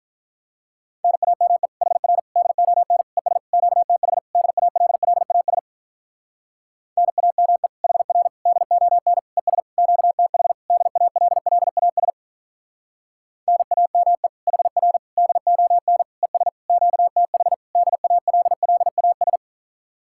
Morse code, heard as a keyed tone, about 43 wpm, 700 Hz. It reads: NAME HR DON ES QTH DALLAS NAME HR DON ES QTH DALLAS NAME HR DON ES QTH DALLAS
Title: Speed-Racing Common QSO Phrases - 35wpm